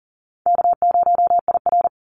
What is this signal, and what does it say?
Morse code, heard as a keyed tone, about 40 wpm, 700 Hz. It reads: K0IR
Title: Easy Call Signs Repeated 2x - 40wpm